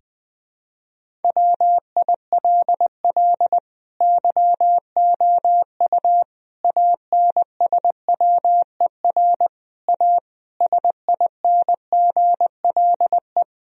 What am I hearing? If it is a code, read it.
Morse code, heard as a keyed tone, about 20 wpm, 700 Hz. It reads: WILL YOU ANSWER A SINGLE
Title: Sentences from Top 1000 Words - Review Part 4 - 20wpm